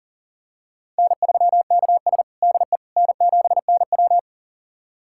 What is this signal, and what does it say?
Morse code, heard as a keyed tone, about 40 wpm, 700 Hz. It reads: N3KS DE N7NW
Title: USA Callsign DE Callsign - 4 Characters - 40wpm